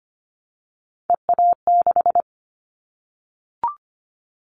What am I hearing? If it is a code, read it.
Morse code, heard as a keyed tone, about 25 wpm, 700 Hz. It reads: EA6
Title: Callsign Prefixes for Top 50 DXCC - 25wpm